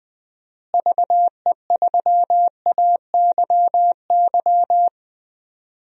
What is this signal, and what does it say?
Morse code, heard as a keyed tone, about 20 wpm, 700 Hz. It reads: VE3AYY
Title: DXCC-100 Call Signs - 6 Characters - Repeated 3x - 20wpm